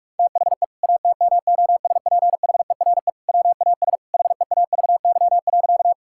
Morse code, heard as a keyed tone, about 46 wpm, 700 Hz. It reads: THE ATMOSPHERE WAS HEAVY.
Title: Sentences from Top 1600 Words 45wpm